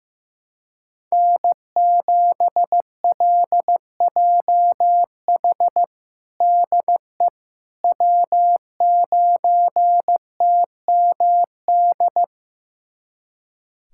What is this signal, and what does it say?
Morse code, heard as a keyed tone, about 15 wpm, 700 Hz. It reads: N7LJH DE W9TMD
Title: USA Callsign DE Callsign - 5 Characters - 15wpm